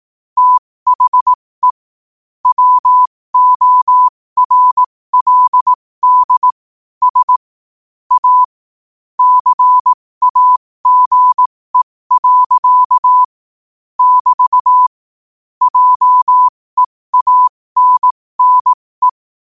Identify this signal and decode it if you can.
Morse code, heard as a keyed tone, about 18 wpm, spaced as for 15 wpm, 1000 Hz. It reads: THE WORLD S A CAGE. <BT> JEANNE
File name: qotdfiles / 15wpm